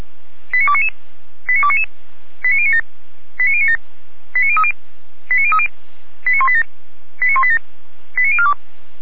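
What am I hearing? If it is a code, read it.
Five-tone selective call (ZVEI), heard as quick runs of five tones, tones of about 70 ms. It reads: 89299 89299 89097 89097 89030 89030 89187 89187 89042
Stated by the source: Five-tone ZVEI selective call 89294 received by SDR.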